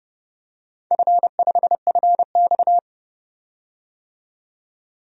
Morse code, heard as a keyed tone, about 30 wpm, 700 Hz. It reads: F5FX
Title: DXCC-100 Call Signs - 4 Characters - 30wpm